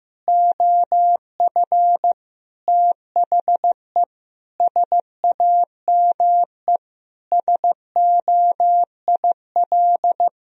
Morse code, heard as a keyed tone, about 16 wpm, 700 Hz. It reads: OF THE SAME SOIL
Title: Sentences from Top 1000 Words - Review Part 4 - 15wpm